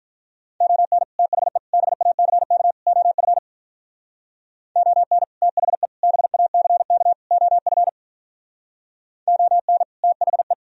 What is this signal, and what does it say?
Morse code, heard as a keyed tone, about 44 wpm, 700 Hz. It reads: ON THE BACK OF ON THE BACK OF ON THE